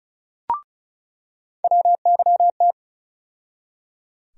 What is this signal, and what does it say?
Morse code, heard as a keyed tone, about 36 wpm, 700 Hz. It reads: WYT
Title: USA Call Sign Postfixes - 3 Letters 35wpm